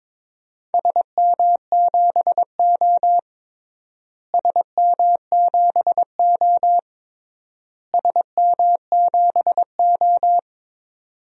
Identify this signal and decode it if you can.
Morse code, heard as a keyed tone, about 22 wpm, 700 Hz. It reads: SM7O SM7O SM7O
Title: DXCC-100 Call Signs - 4 Characters - Repeated 3x - 22wpm